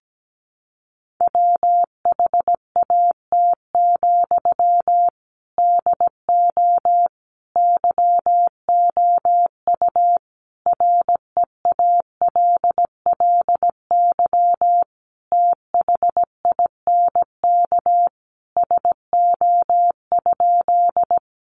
Morse code, heard as a keyed tone, about 17 wpm, 700 Hz. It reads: WHAT, DO YOU REALLY THINK SO?
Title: Sentences from Top 600 Words 17wpm